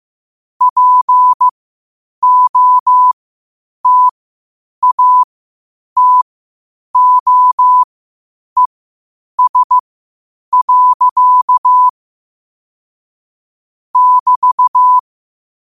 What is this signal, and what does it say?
Morse code, heard as a keyed tone, about 15 wpm, spaced as for 8 wpm, 1000 Hz. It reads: POTATOES. <BT>
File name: qotdfiles / 8wpm